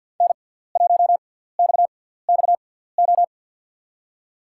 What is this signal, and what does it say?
Morse code, heard as a keyed tone, about 52 wpm, spaced as for 17 wpm, 700 Hz. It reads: N1XXO
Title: USA Call Signs - 5 Characters - 6x Character Spacing - 50wpm